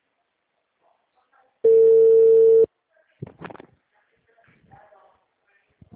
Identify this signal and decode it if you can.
Ringback tone (China, 450 Hz).
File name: out- -2005- - - .23408